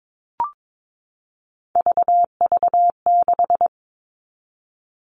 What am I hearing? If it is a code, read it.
Morse code, heard as a keyed tone, about 22 wpm, 700 Hz. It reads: VV6
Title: All Cut Numbers - 3-Digit 22wpm